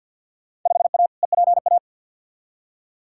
Morse code, heard as a keyed tone, about 47 wpm, 700 Hz. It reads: 5A EPA
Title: Field Day Exchanges - 50wpm